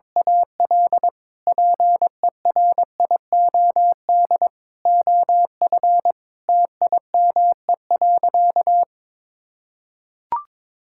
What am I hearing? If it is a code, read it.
Morse code, heard as a keyed tone, about 22 wpm, 700 Hz. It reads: AL PERIOD OF TIME.
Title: Sentences from Top 1000 Words - Review Part 5 - 22wpm